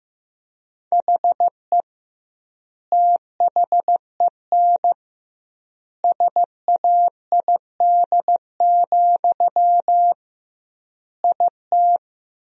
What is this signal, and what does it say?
Morse code, heard as a keyed tone, about 15 wpm, 700 Hz. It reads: HE THEN SAID, IT
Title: 2x Word-Spacing - Sentences from Top 300 Words - 15wpm